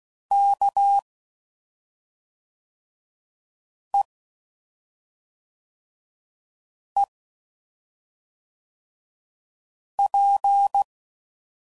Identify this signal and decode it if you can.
Morse code, heard as a keyed tone, about 16 wpm, 800 Hz. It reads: K E E P
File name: Suffixes / Special ING's / SlowToFaster ING's